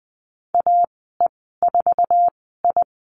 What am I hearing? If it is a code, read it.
Morse code, heard as a keyed tone, about 20 wpm, spaced as for 15 wpm, 700 Hz. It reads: AE4I
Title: 186 Lesson 30g - Letter b - Call Signs 20wpm